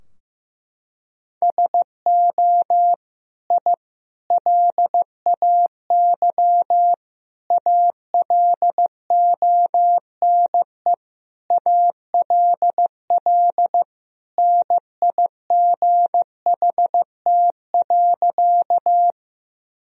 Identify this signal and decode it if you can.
Morse code, heard as a keyed tone, about 15 wpm, 700 Hz. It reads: SO I LAY ALONE ALL NIGHT.
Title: Sentences from Top 1000 Words - Review Part 2 - 15wpm